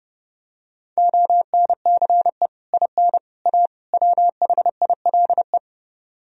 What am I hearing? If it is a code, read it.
Morse code, heard as a keyed tone, about 30 wpm, 700 Hz. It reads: ONCE IN A WHILE